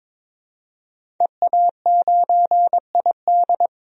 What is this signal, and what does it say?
Morse code, heard as a keyed tone, about 22 wpm, 700 Hz. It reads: EA9ID
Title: DXCC-100 Call Signs - 5 Characters - 22wpm